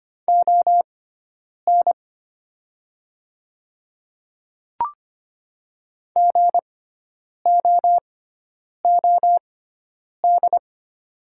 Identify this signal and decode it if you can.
Morse code, heard as a keyed tone, about 25 wpm, spaced as for 9 wpm, 700 Hz. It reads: ON GOOD
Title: 6x Character Spacing - Top 200 Words - 25wpm